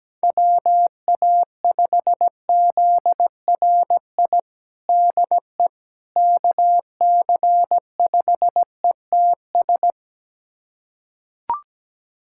Morse code, heard as a keyed tone, about 17 wpm, 700 Hz. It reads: WA5ZRI DE KC5ETS
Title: USA Callsign DE Callsign 17wpm